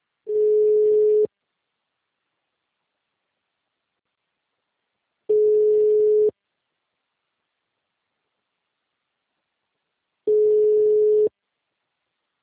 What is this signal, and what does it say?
Ringback tone (Europe, 425 Hz), with speech.
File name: out- -2004- - - .50245